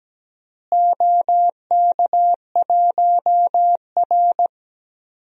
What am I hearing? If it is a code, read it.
Morse code, heard as a keyed tone, about 17 wpm, 700 Hz. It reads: OK1R
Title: DXCC-100 Call Signs - 4 Characters - Repeated 3x - 17wpm